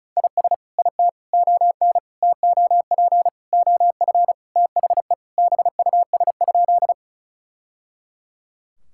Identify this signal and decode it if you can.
Morse code, heard as a keyed tone, about 34 wpm, 700 Hz. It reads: IS IT ON TOP OF THE BUS?
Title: Sentences from Top 1000 Words - Review Part 6 - 35wpm